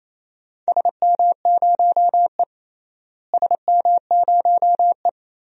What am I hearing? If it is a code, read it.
Morse code, heard as a keyed tone, about 28 wpm, 700 Hz. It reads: SM0E SM0E
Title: DXCC-100 Call Signs - 4 Characters - Repeated 3x - 28wpm